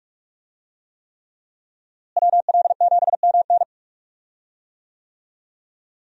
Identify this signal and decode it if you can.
Morse code, heard as a keyed tone, about 46 wpm, 700 Hz. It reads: WL7MN
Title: USA Call Signs - 5 Characters - Repeated 3x - 45wpm